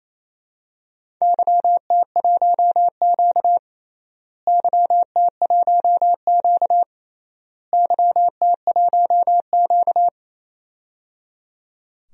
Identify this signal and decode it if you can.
Morse code, heard as a keyed tone, about 28 wpm, 700 Hz. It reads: YT1Q YT1Q YT1Q
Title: DXCC-100 Call Signs - 4 Characters - Repeated 3x - 28wpm